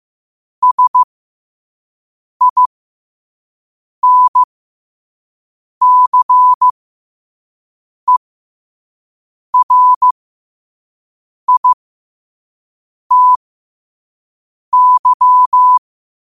Morse code, heard as a keyed tone, about 15 wpm, spaced as for 5 wpm, 1000 Hz. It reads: SINCERITY
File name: qotdfiles / 5wpm